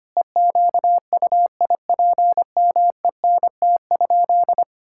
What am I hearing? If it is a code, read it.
Morse code, heard as a keyed tone, about 25 wpm, 700 Hz. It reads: EQUIPMENT?
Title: Sentences from Top 1000 Words - Review Part 6 - 25wpm